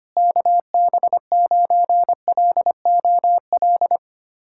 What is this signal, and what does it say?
Morse code, heard as a keyed tone, about 25 wpm, 700 Hz. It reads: KB9LOL
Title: USA Callsign DE Callsign - 6 Characters - 25wpm